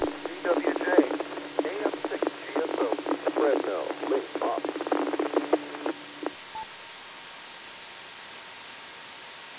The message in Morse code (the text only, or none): none